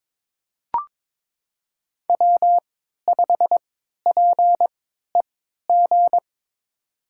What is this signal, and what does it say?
Morse code, heard as a keyed tone, about 22 wpm, spaced as for 13 wpm, 700 Hz. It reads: W5PEG
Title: USA Call Signs - 3x Character Spacing - 22wpm